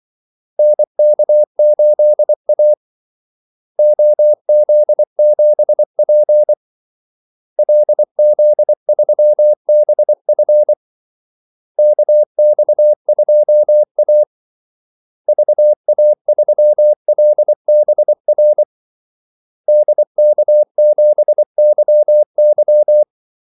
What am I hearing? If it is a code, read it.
Morse code, heard as a keyed tone, about 24 wpm, 600 Hz. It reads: NK8A OZ7P LZ3BF KX2A VA3LBR DK7YY
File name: Callsigns-24wpm